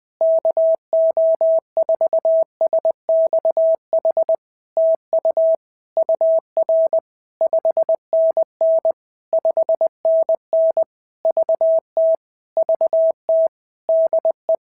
Morse code, heard as a keyed tone, about 20 wpm, 650 Hz. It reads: KO4SXH TU UR 5NN 5NN VT VT DE
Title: POTA QSO Practice - 20wpm